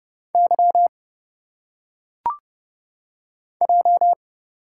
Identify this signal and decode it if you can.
Morse code, heard as a keyed tone, about 30 wpm, 700 Hz. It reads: Y J
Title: Single Letters - Phonetic Pronunciation 30wpm